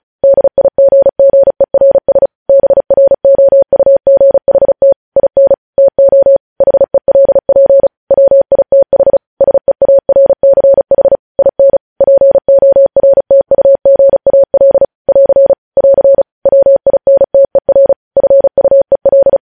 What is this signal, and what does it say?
Morse code, heard as a keyed tone, about 36 wpm, 550 Hz. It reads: DIGGERS BROUGHT IN TO HELP WITH SEARCH IN PORTUGAL <AR> <AR> WINTER FUEL
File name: podcast / News headlines-1x-35x35